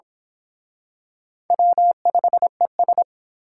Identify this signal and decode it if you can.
Morse code, heard as a keyed tone, about 27 wpm, 700 Hz. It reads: W5ES
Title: Speed Racing - USA 1x2 Call Signs 22wpm